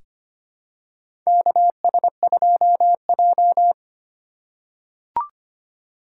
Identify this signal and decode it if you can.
Morse code, heard as a keyed tone, about 25 wpm, 700 Hz. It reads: KS2J
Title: USA Call Signs - 4 Characters - 2x Character Spacing - 25wpm